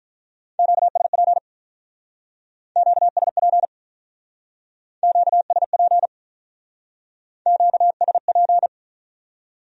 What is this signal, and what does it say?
Morse code, heard as a keyed tone, about 44 wpm, 700 Hz. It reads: QSP QSP QSP QSP